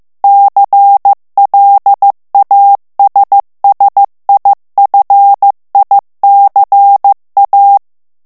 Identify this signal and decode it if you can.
Morse code, heard as a keyed tone, about 15 wpm, 800 Hz. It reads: CLASSIFICA
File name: morse mp3 / ITALIEN texte clair 2 vitesse 14 mots minute